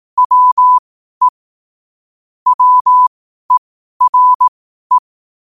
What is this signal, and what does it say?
Morse code, heard as a keyed tone, about 18 wpm, spaced as for 12 wpm, 1000 Hz. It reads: WE WERE
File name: qotdfiles / 12wpm